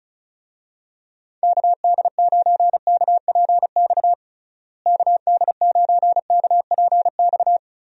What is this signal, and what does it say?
Morse code, heard as a keyed tone, about 34 wpm, 700 Hz. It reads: KD9KPX KD9KPX
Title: USA Call Signs - 6 Characters - Repeated 3x - 35wpm